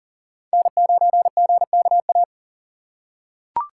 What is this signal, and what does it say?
Morse code, heard as a keyed tone, about 40 wpm, 700 Hz. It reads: N9GKA
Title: USA Call Signs - 5 Characters - 3x Character Spacing - 40wpm